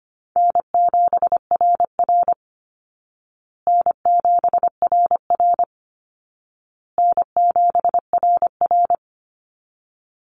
Morse code, heard as a keyed tone, about 25 wpm, 700 Hz. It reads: N7RR N7RR N7RR
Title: Call Signs Repeated 2x - 25wpm